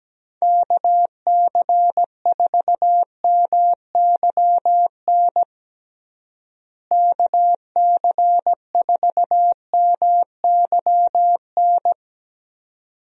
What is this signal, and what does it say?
Morse code, heard as a keyed tone, about 17 wpm, 700 Hz. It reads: KC4MYN KC4MYN
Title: USA Call Signs - 6 Characters - Repeated 2x - 17wpm